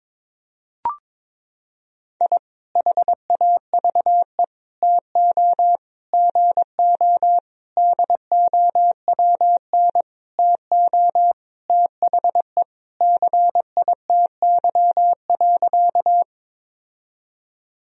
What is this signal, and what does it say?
Morse code, heard as a keyed tone, about 21 wpm, 700 Hz. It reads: I HAVE TO GO DOWN TO THE CITY.
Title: Sentences from Top 1000 Words - Review Part 1 - 22wpm